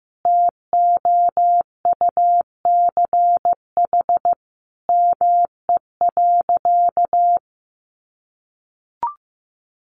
Morse code, heard as a keyed tone, about 15 wpm, 700 Hz. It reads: TOUCH ME.
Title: Sentences from Top 1200 Words 15wpm